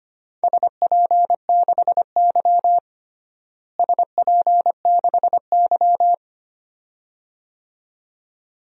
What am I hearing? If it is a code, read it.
Morse code, heard as a keyed tone, about 25 wpm, 700 Hz. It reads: SP6Y SP6Y
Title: DXCC-100 Call Signs - 4 Characters - Repeated 2x - 25wpm